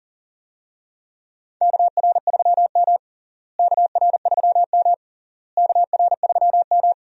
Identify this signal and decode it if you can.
Morse code, heard as a keyed tone, about 40 wpm, 700 Hz. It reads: KR3M KR3M KR3M
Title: USA Call Signs - 4 Characters - Repeated 3x - 40wpm